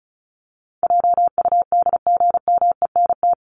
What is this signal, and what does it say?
Morse code, heard as a keyed tone, about 34 wpm, 700 Hz. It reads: JUDGMENT
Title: Top 1751-2000 Words 35wpm